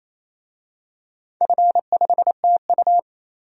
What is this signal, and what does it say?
Morse code, heard as a keyed tone, about 29 wpm, 690 Hz. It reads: F5TU
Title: DXCC-100 Call Signs - 4 Characters - 28wpm